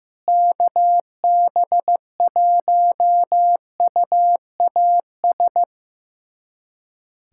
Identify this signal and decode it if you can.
Morse code, heard as a keyed tone, about 15 wpm, 700 Hz. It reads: KB1UAS
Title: USA Callsign DE Callsign - 6 Characters - 15wpm